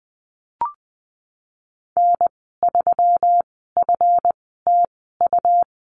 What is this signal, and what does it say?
Morse code, heard as a keyed tone, about 20 wpm, spaced as for 15 wpm, 700 Hz. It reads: N3FTU
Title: 140 Lesson 23g - Letter f - Call Signs 20wpm